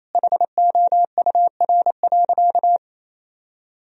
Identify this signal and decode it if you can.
Morse code, heard as a keyed tone, about 28 wpm, 700 Hz. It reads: HOUR.
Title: Sentences from Top 1000 Words - Review Part 1 - 28wpm